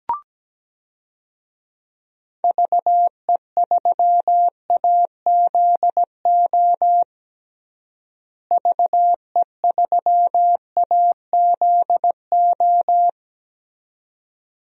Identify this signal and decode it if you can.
Morse code, heard as a keyed tone, about 17 wpm, 700 Hz. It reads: VE3AZO VE3AZO
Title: DXCC-100 Call Signs - 6 Characters - Repeated 2x - 17wpm